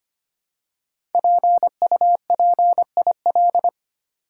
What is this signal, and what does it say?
Morse code, heard as a keyed tone, about 25 wpm, 700 Hz. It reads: PUPIL
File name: 5 Letter Words - First 3 Letters Given - 25wpm